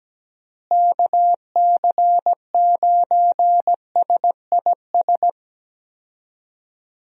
Morse code, heard as a keyed tone, about 17 wpm, 700 Hz. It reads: KC9SIS
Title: USA Call Signs - 6 Characters - Repeated 2x - 17wpm